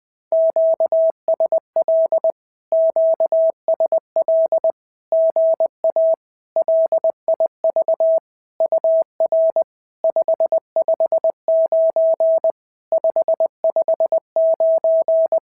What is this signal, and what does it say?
Morse code, heard as a keyed tone, about 20 wpm, 650 Hz. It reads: QSL QSL GA LIV UR 559 559
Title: SOTA QSO Practice 20wpm